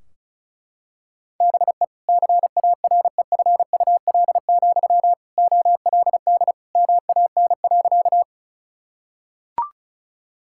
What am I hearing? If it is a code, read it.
Morse code, heard as a keyed tone, about 34 wpm, 700 Hz. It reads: BE CAREFUL, OLD MAN.
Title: Sentences from Top 1600 Words 35wpm